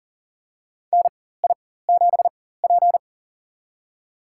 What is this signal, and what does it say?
Morse code, heard as a keyed tone, about 40 wpm, spaced as for 19 wpm, 700 Hz. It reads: NI7P
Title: USA Call Signs - 4 Characters - 4x Character Spacing - 40wpm